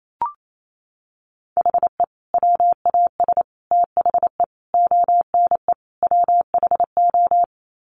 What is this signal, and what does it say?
Morse code, heard as a keyed tone, about 28 wpm, 700 Hz. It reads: HE WAS THE ONE WHO
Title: Sets of 5 Words Encore2 28wpm